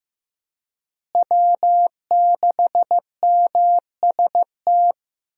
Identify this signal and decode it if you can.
Morse code, heard as a keyed tone, about 15 wpm, 700 Hz. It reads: W6MST
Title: USA Call Signs - 5 Characters - Repeated 3x - 15wpm